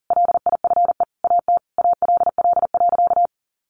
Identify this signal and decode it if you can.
Morse code, heard as a keyed tone, about 40 wpm, 700 Hz. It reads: LIFE AT ALL.
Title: Sentences from Top 1100 Words 40wpm